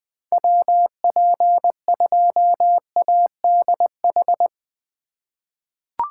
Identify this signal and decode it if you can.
Morse code, heard as a keyed tone, about 21 wpm, 700 Hz. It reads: WP2ADH
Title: USA Call Signs - 4x Character Spacing - 20wpm